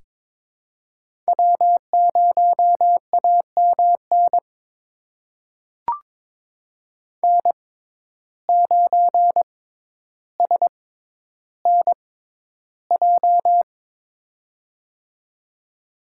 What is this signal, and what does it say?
Morse code, heard as a keyed tone, about 23 wpm, spaced as for 11 wpm, 700 Hz. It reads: W0AMN N9SNJ
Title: USA Call Signs - 5 Characters - 6x Character Spacing - 22wpm